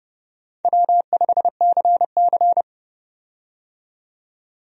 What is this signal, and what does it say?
Morse code, heard as a keyed tone, about 30 wpm, 700 Hz. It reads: W5CC
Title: USA Call Signs - 4 Characters - 30wpm